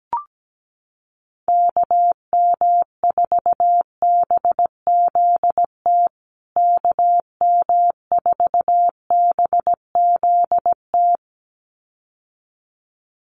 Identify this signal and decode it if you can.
Morse code, heard as a keyed tone, about 17 wpm, 700 Hz. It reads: KM4BZT KM4BZT
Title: USA Call Signs Repeated 2x 17wpm